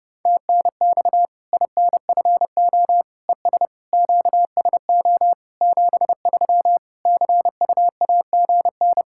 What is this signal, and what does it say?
Morse code, heard as a keyed tone, about 30 wpm, 700 Hz. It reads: TNX INFO ES QSO 73 CUAGN
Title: Practice QSO - Exchange 4 - Ending - 30wpm